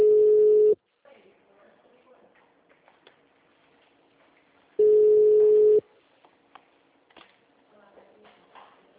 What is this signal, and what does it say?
Ringback tone (Europe, 425 Hz), with speech.